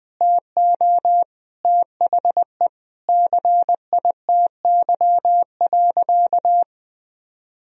Morse code, heard as a keyed tone, about 20 wpm, 700 Hz. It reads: TO THE CITY.
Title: Sentences from Top 1000 Words - Review Part 1 - 20wpm